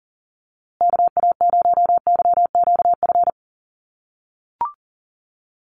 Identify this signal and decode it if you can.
Morse code, heard as a keyed tone, about 40 wpm, 700 Hz. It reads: KA0YQF
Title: USA Call Signs Repeated 2x 40wpm